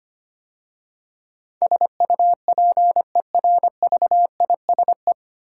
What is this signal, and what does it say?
Morse code, heard as a keyed tone, about 25 wpm, 700 Hz. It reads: SUPERVISE
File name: practice / 25wpm / 9 Letter Words - First 3 Letters Given - 25wpm